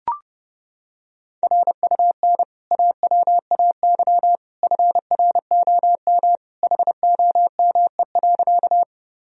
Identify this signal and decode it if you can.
Morse code, heard as a keyed tone, about 30 wpm, 700 Hz. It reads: RUN AWAY FROM HOME.
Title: Sentences from Top 1000 Words - Review Part 1 - 30wpm